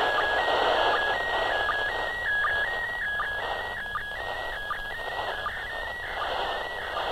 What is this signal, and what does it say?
SSTV (slow-scan television), PD 180.
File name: ISS 2015-04-11 145 800MHz SSTV